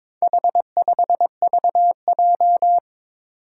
Morse code, heard as a keyed tone, about 22 wpm, 700 Hz. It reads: H5VJ
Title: Four Challenging Letter-Number Combinations 22wpm